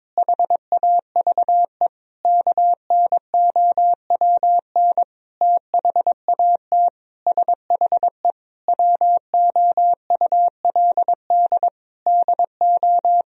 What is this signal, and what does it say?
Morse code, heard as a keyed tone, about 21 wpm, 700 Hz. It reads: HAVE KNOWN THAT SHE WOULD DO
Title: Sentences from Top 1000 Words - Review Part 3 - 22wpm